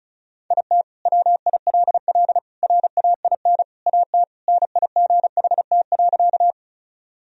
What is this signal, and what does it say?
Morse code, heard as a keyed tone, about 34 wpm, 700 Hz. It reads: IT WILL RAIN AT NIGHT.
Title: Sentences from Top 1000 Words - Review Part 1 - 35wpm